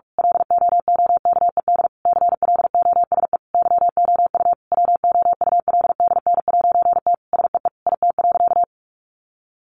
Morse code, heard as a keyed tone, about 44 wpm, 700 Hz. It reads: LOOKED CLOSE YOU WOULDN'T SEE IT.
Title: A Sound Of Thunder 45wpm